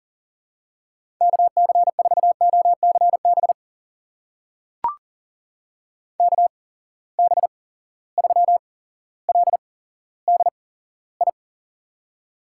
Morse code, heard as a keyed tone, about 40 wpm, spaced as for 17 wpm, 700 Hz. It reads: KC4OCB KB3LDI
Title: USA Call Signs - 6 Characters - 8x Character Spacing - 40wpm